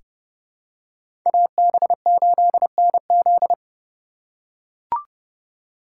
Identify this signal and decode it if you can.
Morse code, heard as a keyed tone, about 30 wpm, 700 Hz. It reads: AB8NZ
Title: USA Call Signs - 5 Characters - 4x Character Spacing - 30wpm